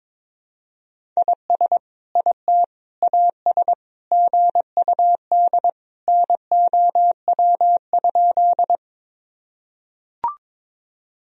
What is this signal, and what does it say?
Morse code, heard as a keyed tone, about 22 wpm, 700 Hz. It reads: IS IT AS GUD NOW?
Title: Sentences from Top 200 Words and Abbreviations 22wpm